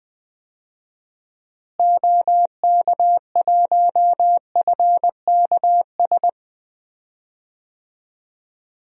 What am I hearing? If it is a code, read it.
Morse code, heard as a keyed tone, about 20 wpm, 700 Hz. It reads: OK1FKS
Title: DXCC-100 Call Signs - 6 Characters - Repeated 2x - 20wpm